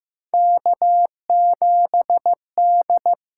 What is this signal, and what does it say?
Morse code, heard as a keyed tone, about 15 wpm, 700 Hz. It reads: K7D
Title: USA Call Signs - 4 Characters - 6x Character Spacing - 15wpm